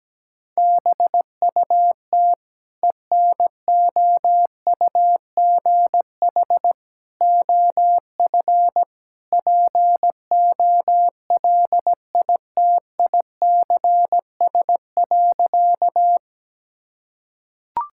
Morse code, heard as a keyed tone, about 17 wpm, 700 Hz. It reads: BUT ENOUGH OF POLITICS.
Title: Sentences from Top 1000 Words - Review Part 6 - 17wpm